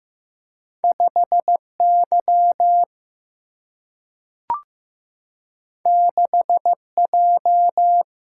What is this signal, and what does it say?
Morse code, heard as a keyed tone, about 15 wpm, 700 Hz. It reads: 5Y 6J
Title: Single-Letter Single-Number 15wpm